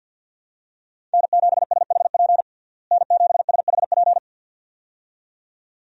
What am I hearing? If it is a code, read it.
Morse code, heard as a keyed tone, about 48 wpm, 700 Hz. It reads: N7SHP N7SHP
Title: USA Call Signs - 5 Characters - Repeated 2x - 50wpm